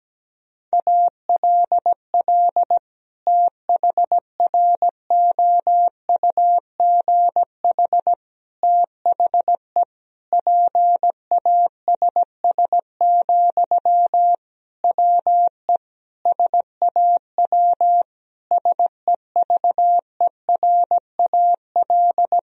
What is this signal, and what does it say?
Morse code, heard as a keyed tone, about 17 wpm, 700 Hz. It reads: ALL THROUGH THE PASS, WE SAW SEVERAL
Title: Sentences from Top 1000 Words - Review Part 2 - 17wpm